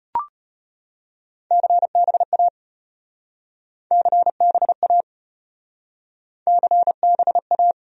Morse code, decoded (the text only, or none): CBA CBA CBA